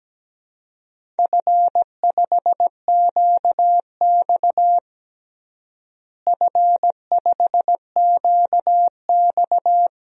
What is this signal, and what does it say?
Morse code, heard as a keyed tone, about 17 wpm, 700 Hz. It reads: F5QX F5QX
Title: DXCC-100 Call Signs - 4 Characters - Repeated 2x - 17wpm